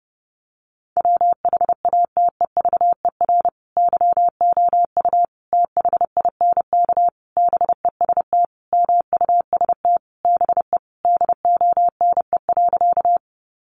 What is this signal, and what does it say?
Morse code, heard as a keyed tone, about 30 wpm, 700 Hz. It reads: WHATEVER YOU THINK BEST MUST BE DONE.
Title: Sentences from Top 1100 Words 30wpm